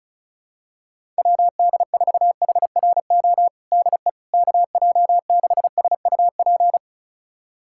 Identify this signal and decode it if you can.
Morse code, heard as a keyed tone, about 36 wpm, 700 Hz. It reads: WD4HRO DE KJ6SUP
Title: USA Callsign DE Callsign 35wpm